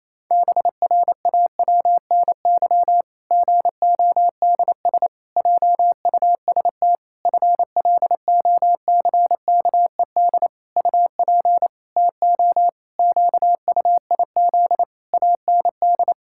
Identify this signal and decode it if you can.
Morse code, heard as a keyed tone, about 27 wpm, 700 Hz. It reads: BRAWNY GODS JUST FLOCKED UP TO QUIZ AND
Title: Pangram Challenge v2 - 28wpm